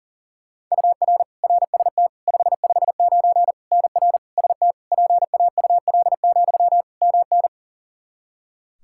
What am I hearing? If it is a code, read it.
Morse code, heard as a keyed tone, about 37 wpm, 700 Hz. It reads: UR RST 559 NR ST PAUL, MN
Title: Practice QSO - Exchange 1 - Beginning - 40wpm